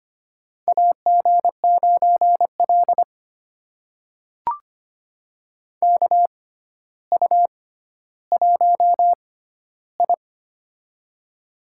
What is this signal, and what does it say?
Morse code, heard as a keyed tone, about 25 wpm, spaced as for 13 wpm, 700 Hz. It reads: AG9L KU1I
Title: USA Call Signs - 4 Characters - 6x Character Spacing - 25wpm